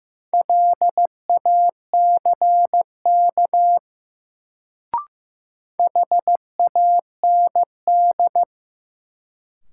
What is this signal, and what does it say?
Morse code, heard as a keyed tone, about 15 wpm, 700 Hz. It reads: LACK HAND
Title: Sets of 3 Words - New Word Drill - 15wpm